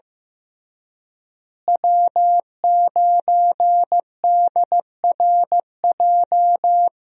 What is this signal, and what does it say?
Morse code, heard as a keyed tone, about 15 wpm, 700 Hz. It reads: W9DRJ
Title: USA Call Signs - 5 Characters - Repeated 3x - 15wpm